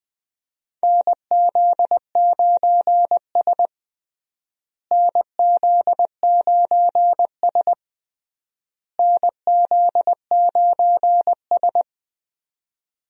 Morse code, heard as a keyed tone, about 20 wpm, 700 Hz. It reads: NZ9S NZ9S NZ9S
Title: USA Call Signs - 4 Characters - Repeated 3x - 20wpm